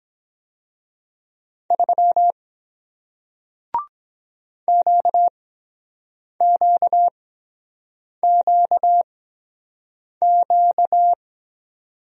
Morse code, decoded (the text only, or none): 3 QQQQ